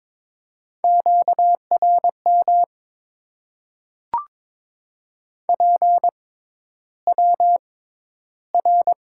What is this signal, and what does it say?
Morse code, heard as a keyed tone, about 22 wpm, spaced as for 11 wpm, 700 Hz. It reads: QRM PWR
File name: Top 90 QSO Elements - 6x Character Spacing - 22wpm